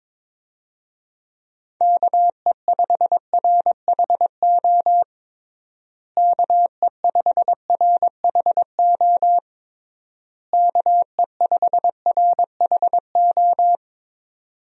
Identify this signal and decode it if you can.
Morse code, heard as a keyed tone, about 22 wpm, 700 Hz. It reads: KE5RHO KE5RHO KE5RHO
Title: USA Call Signs - 6 Characters - Repeated 3x - 22wpm